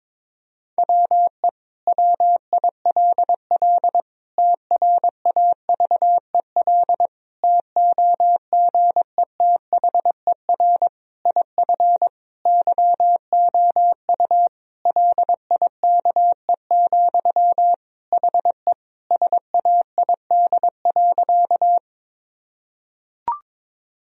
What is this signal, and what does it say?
Morse code, heard as a keyed tone, about 22 wpm, 700 Hz. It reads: WE WILL TRAVEL TOGETHER IF YOU LIKE, HE SAID.
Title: Sentences from Top 1000 Words - Review Part 2 - 22wpm